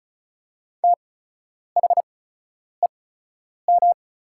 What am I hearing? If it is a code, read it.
Morse code, heard as a keyed tone, about 34 wpm, spaced as for 10 wpm, 690 Hz. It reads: THEM
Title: 8x Character Spacing - Top 200 Words - 35wpm